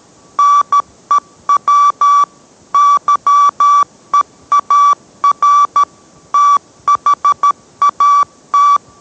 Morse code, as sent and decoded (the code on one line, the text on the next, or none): -. .-- / -.-- . .- .-. / - .... .- -
NEW YEAR THAT